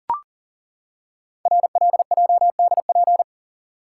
Morse code, heard as a keyed tone, about 38 wpm, 690 Hz. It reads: RLJDP
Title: Code Groups - 5 Letters - 40wpm